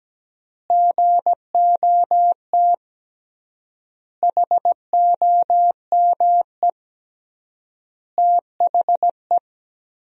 Morse code, heard as a keyed tone, about 17 wpm, 700 Hz. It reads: GOT HOME THE
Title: 3x Word-Spacing - Sentences from Top 500 Words - 17wpm